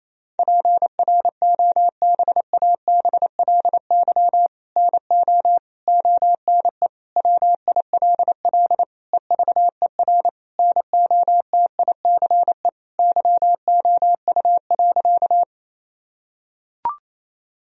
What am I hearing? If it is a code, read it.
Morse code, heard as a keyed tone, about 28 wpm, 700 Hz. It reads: PROBABLY NO ONE WILL EVER NOTICE YOU.
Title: Sentences from Top 1000 Words - Review Part 4 - 28wpm